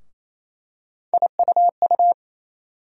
Morse code, heard as a keyed tone, about 28 wpm, 700 Hz. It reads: IUU
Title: USA Call Sign Postfixes - 3 Letters 28wpm